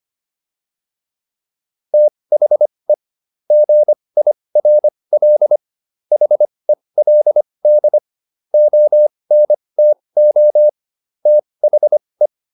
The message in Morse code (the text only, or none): THE GIRL HELD ONTO THE